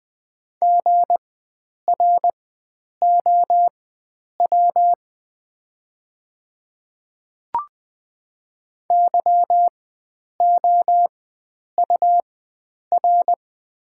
Morse code, heard as a keyed tone, about 20 wpm, spaced as for 9 wpm, 700 Hz. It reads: GROW YOUR
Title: 4x Character Spacing - Top 500 Words - 20wpm